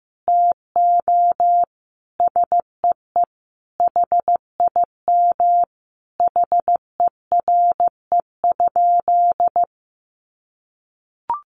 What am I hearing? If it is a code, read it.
Morse code, heard as a keyed tone, about 14 wpm, 700 Hz. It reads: TO SEE HIM HERE?
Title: Sentences from Top 1100 Words 15wpm